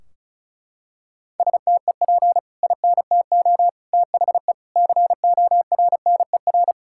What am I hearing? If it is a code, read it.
Morse code, heard as a keyed tone, about 34 wpm, 700 Hz. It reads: STEP INTO THE CORNER
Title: Sentences from Top 1600 Words 35wpm